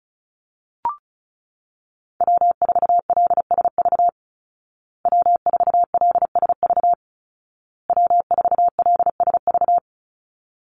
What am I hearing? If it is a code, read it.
Morse code, heard as a keyed tone, about 34 wpm, 700 Hz. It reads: W4LSV W4LSV W4LSV
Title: Call Signs Repeated 2x - 35wpm